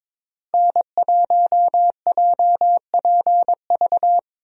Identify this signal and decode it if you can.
Morse code, heard as a keyed tone, about 22 wpm, 700 Hz. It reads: N1JPV
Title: USA Call Signs - 5 Characters - Repeated 3x - 22wpm